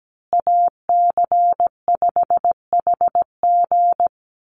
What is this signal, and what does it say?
Morse code, heard as a keyed tone, about 17 wpm, 700 Hz. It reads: AC5HG
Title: USA Call Signs Repeated 2x 17wpm